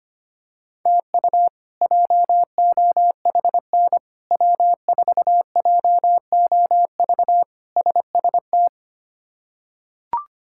Morse code, heard as a keyed tone, about 25 wpm, 700 Hz. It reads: TU JOHN W4JOV SST
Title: K1USN SST QSO Practice - 25wpm